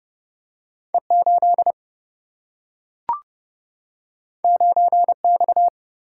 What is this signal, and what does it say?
Morse code, heard as a keyed tone, about 30 wpm, 700 Hz. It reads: E8 9X
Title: Single-Letter Single-Number 30wpm